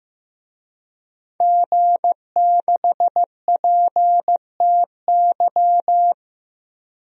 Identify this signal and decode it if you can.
Morse code, heard as a keyed tone, about 15 wpm, 700 Hz. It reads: G6PTY
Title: DXCC-100 Call Signs - 5 Characters - Repeated 2x - 15wpm